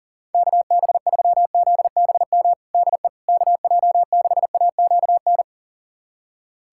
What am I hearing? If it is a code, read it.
Morse code, heard as a keyed tone, about 40 wpm, 690 Hz. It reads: KB3ZBM DE KJ6AQN
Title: USA Callsign DE Callsign - 6 Characters - 40wpm